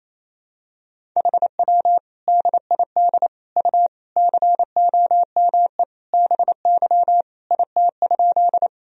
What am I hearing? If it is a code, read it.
Morse code, heard as a keyed tone, about 28 wpm, 700 Hz. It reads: HW DID U COME BY IT?
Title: Sentences from Top 100 Words and Abbreviations 28wpm